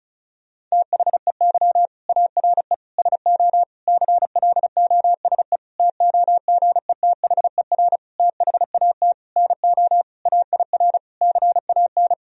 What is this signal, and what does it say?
Morse code, heard as a keyed tone, about 36 wpm, 700 Hz. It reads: THEY ARE SO CLOSE TOGETHER THAT NO AIR CAN
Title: Sentences from Top 1000 Words - Review Part 1 - 35wpm